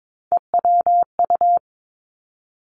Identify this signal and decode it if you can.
Morse code, heard as a keyed tone, about 22 wpm, 700 Hz. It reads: EWU
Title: Three Challenging Letter-Number Combinations 22wpm